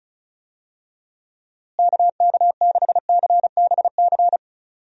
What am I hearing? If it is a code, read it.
Morse code, heard as a keyed tone, about 34 wpm, 700 Hz. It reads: KK6CBC
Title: USA Call Signs - 6 Characters - Repeated 3x - 35wpm